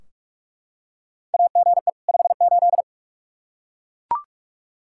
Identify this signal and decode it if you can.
Morse code, heard as a keyed tone, about 41 wpm, 700 Hz. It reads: AGE 58
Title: Practice QSO - Exchange 3 - More Info - 45wpm